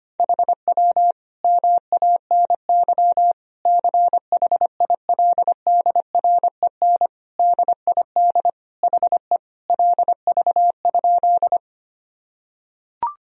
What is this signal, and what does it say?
Morse code, heard as a keyed tone, about 25 wpm, 700 Hz. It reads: HW MANY CHILDREN DID HE LV?
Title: Sentences from Top 400 Words and Abbreviations 25wpm